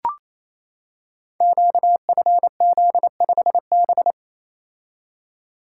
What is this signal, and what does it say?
Morse code, heard as a keyed tone, about 27 wpm, 700 Hz. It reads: QFZ5B
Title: Code Groups - 5 Letter:Numbers - 28wpm